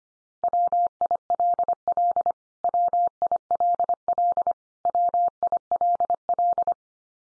Morse code, heard as a keyed tone, about 25 wpm, 700 Hz. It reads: WILL WILL WILL
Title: Top 100 Words 25wpm x3